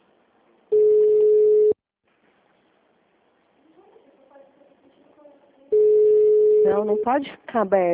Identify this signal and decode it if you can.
Ringback tone (Europe, 425 Hz), with speech.